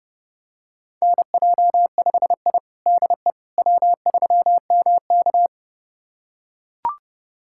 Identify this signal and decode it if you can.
Morse code, heard as a keyed tone, about 30 wpm, 700 Hz. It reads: NJ5I DE W3MK
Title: USA Callsign DE Callsign - 4 Characters - 30wpm